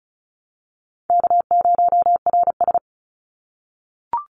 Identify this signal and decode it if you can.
Morse code, heard as a keyed tone, about 36 wpm, 700 Hz. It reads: K0RS
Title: NA SOTA Chaser Callsigns 35wpm